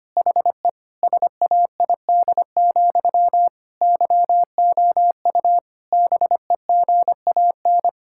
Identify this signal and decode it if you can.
Morse code, heard as a keyed tone, about 25 wpm, 700 Hz. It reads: HE SAID, YOU BEGAN
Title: Sentences from Top 1600 Words 25wpm